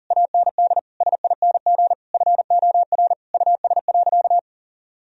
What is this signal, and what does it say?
Morse code, heard as a keyed tone, about 40 wpm, 700 Hz. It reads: AND SING FOR US.
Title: Sentences from Top 1000 Words - Review Part 2 - 40wpm